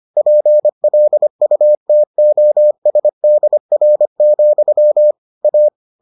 Morse code, heard as a keyed tone, about 25 wpm, 600 Hz. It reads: PLUTOSDR, A